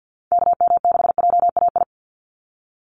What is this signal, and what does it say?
Morse code, heard as a keyed tone, about 49 wpm, 700 Hz. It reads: KM6JAI
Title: USA Call Signs 50wpm